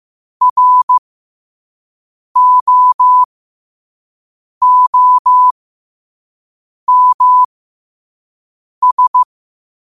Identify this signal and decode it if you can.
Morse code, heard as a keyed tone, about 15 wpm, spaced as for 5 wpm, 1000 Hz. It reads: ROOMS